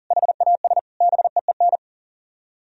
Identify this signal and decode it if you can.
Morse code, heard as a keyed tone, about 52 wpm, 700 Hz. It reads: HAS BEEN